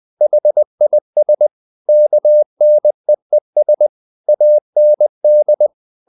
Morse code, heard as a keyed tone, about 20 wpm, 600 Hz. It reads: HIS KNEES AND
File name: the old man and the sea part 2 20wpm